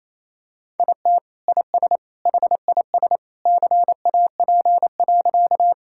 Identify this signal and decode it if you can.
Morse code, heard as a keyed tone, about 28 wpm, 700 Hz. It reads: IT IS HIS CAP.
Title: Sentences from Top 1600 Words 28wpm